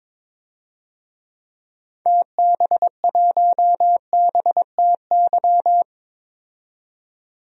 Morse code, heard as a keyed tone, about 22 wpm, 700 Hz. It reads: TB1BTY
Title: DXCC-100 Call Signs - 6 Characters - 22wpm